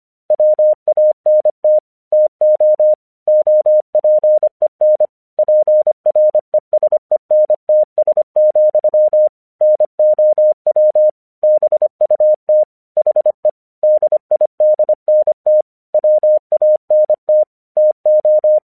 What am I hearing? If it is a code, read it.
Morse code, heard as a keyed tone, about 25 wpm, 610 Hz. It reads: WANT TO OPEN PRESENTS, NOW BUT HE DIDNT WANT TO
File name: 1 English 25wpm / AQC19